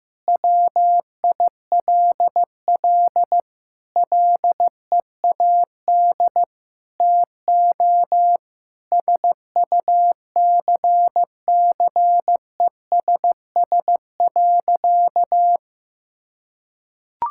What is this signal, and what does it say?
Morse code, heard as a keyed tone, about 15 wpm, 700 Hz. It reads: WILL LEAD TO SUCCESS.
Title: Sentences from Top 1000 Words - Review Part 6 - 15wpm